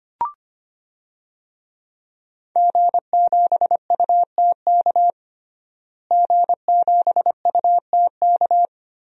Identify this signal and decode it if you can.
Morse code, heard as a keyed tone, about 25 wpm, 700 Hz. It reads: G7UTK G7UTK
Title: DXCC-100 Call Signs - 5 Characters - Repeated 2x - 25wpm